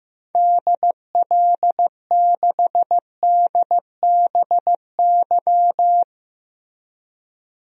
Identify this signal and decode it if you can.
Morse code, heard as a keyed tone, about 16 wpm, 700 Hz. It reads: DL6DBY
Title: DXCC-100 Call Signs - 6 Characters - Repeated 3x - 15wpm